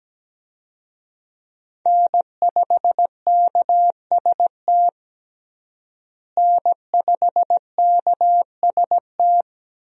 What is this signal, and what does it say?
Morse code, heard as a keyed tone, about 17 wpm, 700 Hz. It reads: N5KST N5KST
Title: USA Call Signs - 5 Characters - Repeated 2x - 17wpm